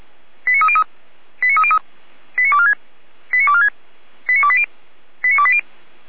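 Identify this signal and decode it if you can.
Five-tone selective call (ZVEI), heard as quick runs of five tones, tones of about 70 ms. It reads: 89303 89303 89246 89246 89280 89280
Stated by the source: ZVEI five-tone selective call received by SDR, fire brigade alerting, Feuerwehr Orlamünde.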